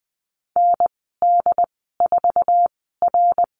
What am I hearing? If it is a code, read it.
Morse code, heard as a keyed tone, about 20 wpm, spaced as for 14 wpm, 700 Hz. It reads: ND4R
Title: 082 Lesson 14g - Number 2 - Call Signs 20wpm